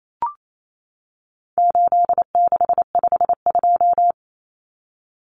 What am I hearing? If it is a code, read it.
Morse code, heard as a keyed tone, about 28 wpm, 700 Hz. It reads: 8652
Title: Common Cut Numbers - 4-Digit 28wpm